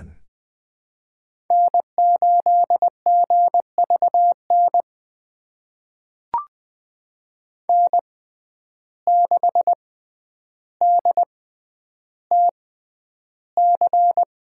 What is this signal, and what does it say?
Morse code, heard as a keyed tone, about 20 wpm, spaced as for 10 wpm, 700 Hz. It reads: N8GVN N6DTC
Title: USA Call Signs - 5 Characters - 6x Character Spacing - 20wpm